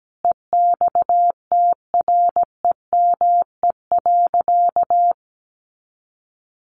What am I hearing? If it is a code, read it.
Morse code, heard as a keyed tone, about 17 wpm, 700 Hz. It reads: EXTREME.
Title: Sentences from Top 1300 Words 17wpm